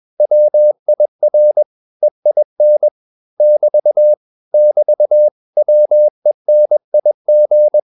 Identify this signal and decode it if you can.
Morse code, heard as a keyed tone, about 21 wpm, 600 Hz. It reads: WIR EIN <BT> <BT> WENIG